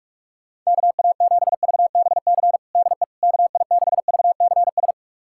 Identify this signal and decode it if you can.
Morse code, heard as a keyed tone, about 46 wpm, 700 Hz. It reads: KA7VBC DE KI6VCS
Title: USA Callsign DE Callsign - 6 Characters - 45wpm